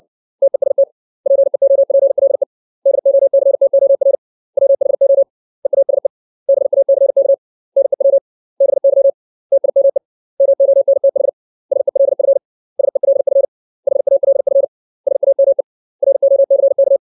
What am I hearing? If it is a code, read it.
Morse code, heard as a keyed tone, about 60 wpm, 550 Hz. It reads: TEST PEOPLE DOCTOR WHO EASE BACK NEW BY TIME MONTH SELF SELF HALF SAME ROCK
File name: 632 words / 632 Words 60WPM